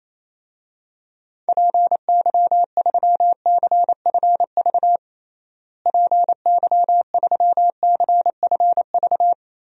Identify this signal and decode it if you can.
Morse code, heard as a keyed tone, about 28 wpm, 700 Hz. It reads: PY3CFV PY3CFV
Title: DXCC-100 Call Signs - 6 Characters - Repeated 2x - 28wpm